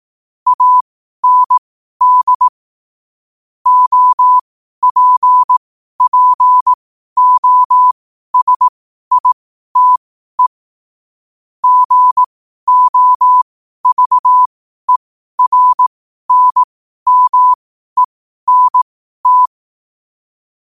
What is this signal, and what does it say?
Morse code, heard as a keyed tone, about 18 wpm, spaced as for 12 wpm, 1000 Hz. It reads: AND OPPOSITE GOVERNMENT
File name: qotdfiles / 12wpm